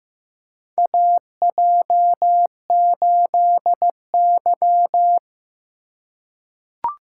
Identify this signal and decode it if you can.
Morse code, heard as a keyed tone, about 15 wpm, 700 Hz. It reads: AJ8Y
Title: USA Call Signs - 4 Characters - 2x Character Spacing - 15wpm